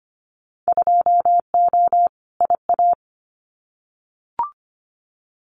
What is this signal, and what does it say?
Morse code, heard as a keyed tone, about 25 wpm, 700 Hz. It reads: 2O IA
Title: Winter Field Day Exchanges 25wpm